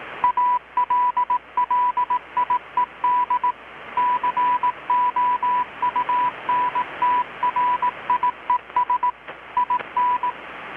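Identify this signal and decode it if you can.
Morse code, heard as a keyed tone, about 18 wpm, 990 Hz. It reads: ALLIED COUNTRIES IN
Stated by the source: Morse from KPH's Night of Nights XX maritime radiotelegraphy event, received in Chiapas, Mexico about 3,800 km away on a Kenwood R-600 with a grounded 10 m long-wire antenna.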